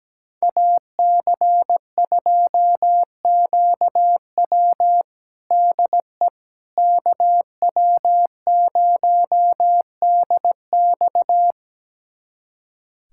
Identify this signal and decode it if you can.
Morse code, heard as a keyed tone, about 17 wpm, 700 Hz. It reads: AC2QW DE KW0DX
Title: USA Callsign DE Callsign - 5 Characters - 17wpm